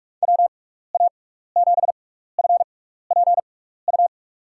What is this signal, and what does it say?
Morse code, heard as a keyed tone, about 45 wpm, spaced as for 15 wpm, 700 Hz. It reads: WA7FPU
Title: USA Call Signs - 6 Characters - 6x Character Spacing - 45wpm